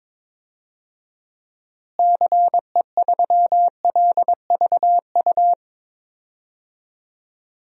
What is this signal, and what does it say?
Morse code, heard as a keyed tone, about 22 wpm, 700 Hz. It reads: CE3LVU
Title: DXCC-100 Call Signs - 6 Characters - Repeated 2x - 22wpm